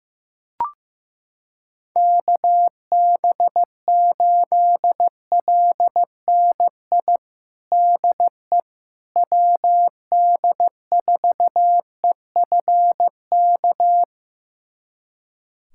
Morse code, heard as a keyed tone, about 15 wpm, 700 Hz. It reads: KB8LNI DE WD4EFK
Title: USA Callsign DE Callsign 15wpm